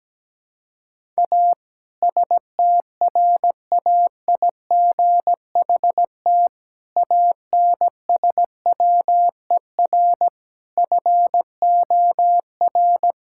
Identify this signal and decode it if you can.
Morse code, heard as a keyed tone, about 17 wpm, 700 Hz. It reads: A STRAIGHT ANSWER FOR
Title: Sentences from Top 1000 Words - Review Part 6 - 17wpm